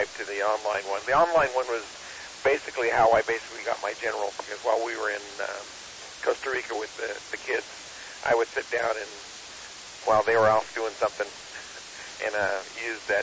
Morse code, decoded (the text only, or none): none